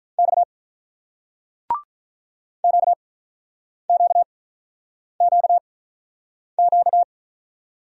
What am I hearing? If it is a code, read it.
Morse code, heard as a keyed tone, about 48 wpm, spaced as for 9 wpm, 700 Hz. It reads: X QQQQ